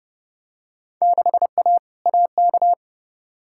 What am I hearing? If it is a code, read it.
Morse code, heard as a keyed tone, about 30 wpm, 700 Hz. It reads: 6A AK
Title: Field Day Exchanges Encore - 30wpm